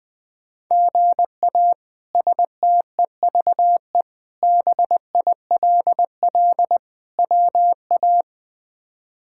Morse code, heard as a keyed tone, about 21 wpm, 700 Hz. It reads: GA STEVE BILL WA
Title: K1USN SST QSO Practice - 20wpm